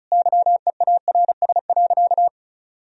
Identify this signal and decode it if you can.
Morse code, heard as a keyed tone, about 34 wpm, 700 Hz. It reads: YEARS.
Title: Sentences from Top 200 Words and Abbreviations 35wpm